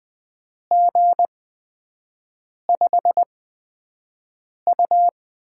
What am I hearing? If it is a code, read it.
Morse code, heard as a keyed tone, about 20 wpm, spaced as for 5 wpm, 700 Hz. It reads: G5U
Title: USA Call Signs - 4 Characters - 8x Character Spacing - 20wpm